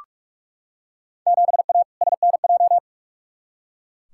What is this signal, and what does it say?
Morse code, heard as a keyed tone, about 42 wpm, 700 Hz. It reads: 7A SNJ
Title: Field Day Exchanges Encore - 45wpm